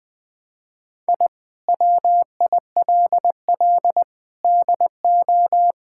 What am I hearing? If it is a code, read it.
Morse code, heard as a keyed tone, about 21 wpm, 700 Hz. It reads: I WILL DO
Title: Sets of 8 Words - 20wpm